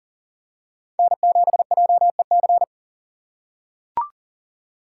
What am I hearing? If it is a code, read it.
Morse code, heard as a keyed tone, about 40 wpm, 700 Hz. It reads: N7JEC
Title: USA Call Signs - 5 Characters - 4x Character Spacing - 40wpm